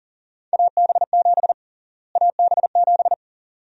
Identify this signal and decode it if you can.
Morse code, heard as a keyed tone, about 40 wpm, 700 Hz. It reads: AB7 AB7
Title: USA Call Sign Prefixes - Repeated 2x - 40wpm